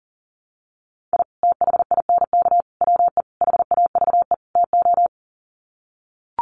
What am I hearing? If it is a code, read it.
Morse code, heard as a keyed tone, about 40 wpm, 700 Hz. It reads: I THINK WE HAVE TO
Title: Sets of 5 Words 40wpm